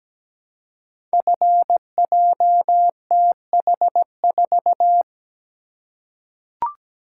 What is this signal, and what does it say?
Morse code, heard as a keyed tone, about 17 wpm, 700 Hz. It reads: FJTH4
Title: Code Groups - 5 Letter:Numbers - 17wpm